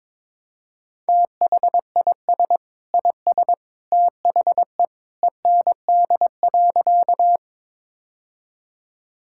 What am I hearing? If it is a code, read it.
Morse code, heard as a keyed tone, about 22 wpm, 700 Hz. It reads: THIS IS THE END.
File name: practice / 22wpm / Sentences with 2-Word Phrases - 22wpm